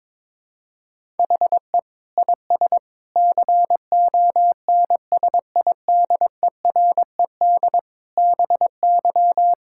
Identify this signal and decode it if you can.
Morse code, heard as a keyed tone, about 22 wpm, 700 Hz. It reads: HE IS CONSIDERED BY
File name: Sentences with 7-Word Phrases - 22wpm